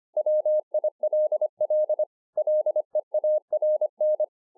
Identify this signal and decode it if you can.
Morse code, heard as a keyed tone, about 25 wpm, 610 Hz. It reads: WILL LEARN